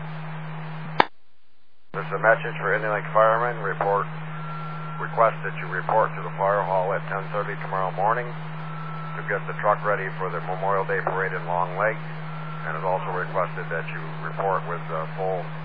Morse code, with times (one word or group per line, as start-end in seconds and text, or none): none